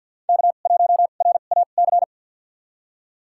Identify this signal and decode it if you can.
Morse code, heard as a keyed tone, about 46 wpm, spaced as for 35 wpm, 700 Hz. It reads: K1RAC
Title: USA Call Signs - 5 Characters - 2x Character Spacing - 50wpm